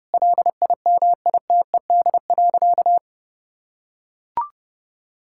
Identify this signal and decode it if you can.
Morse code, heard as a keyed tone, about 30 wpm, 700 Hz. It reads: LIMITED.
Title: Sentences from Top 1000 Words - Review Part 6 - 30wpm